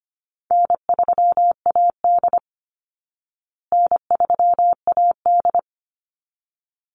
Morse code, heard as a keyed tone, about 25 wpm, 700 Hz. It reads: N3AD N3AD
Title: Easy Call Signs Repeated 2x - 25wpm